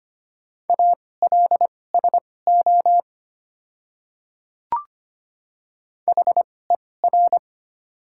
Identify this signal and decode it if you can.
Morse code, heard as a keyed tone, about 25 wpm, spaced as for 18 wpm, 700 Hz. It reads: ALSO HER
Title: 2x Character Spacing - Top 200 Words - 25wpm